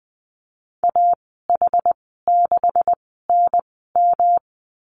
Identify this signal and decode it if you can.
Morse code, heard as a keyed tone, about 20 wpm, spaced as for 15 wpm, 700 Hz. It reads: AH6NM
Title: 231 Lesson 37g - Letter x - Call Signs 20wpm